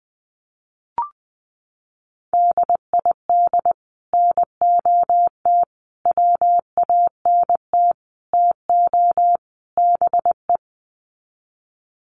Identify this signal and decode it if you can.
Morse code, heard as a keyed tone, about 20 wpm, 700 Hz. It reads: DID NOT WANT TO BE
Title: Sets of 5 Words 20wpm